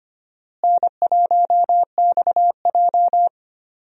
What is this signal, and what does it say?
Morse code, heard as a keyed tone, about 25 wpm, 700 Hz. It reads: N1XJ
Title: USA Call Signs - 4 Characters - Repeated 2x - 25wpm